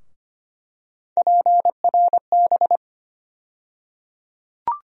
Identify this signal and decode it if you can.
Morse code, heard as a keyed tone, about 25 wpm, 700 Hz. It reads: PRB
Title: USA Call Sign Postfixes - 3 Letters 25wpm